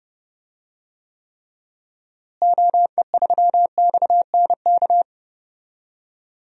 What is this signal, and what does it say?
Morse code, heard as a keyed tone, about 31 wpm, 700 Hz. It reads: OE3XNK
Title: DXCC-100 Call Signs - 6 Characters - Repeated 3x - 30wpm